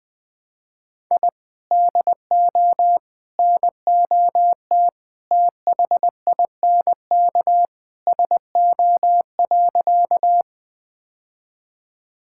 Morse code, heard as a keyed tone, about 20 wpm, 700 Hz. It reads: I DO NOT THINK SO.
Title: Sentences with 4-Word Phrases - 20wpm